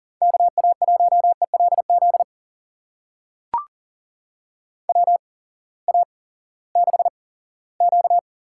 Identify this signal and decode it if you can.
Morse code, heard as a keyed tone, about 40 wpm, spaced as for 20 wpm, 700 Hz. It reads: KA1ELZ WA6Q
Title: USA Call Signs - 6 Characters - 8x Character Spacing - 40wpm